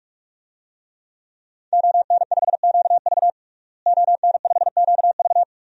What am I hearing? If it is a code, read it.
Morse code, heard as a keyed tone, about 43 wpm, 700 Hz. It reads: ON5QV ON5QV
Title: DXCC-100 Call Signs - 5 Characters - Repeated 2x - 45wpm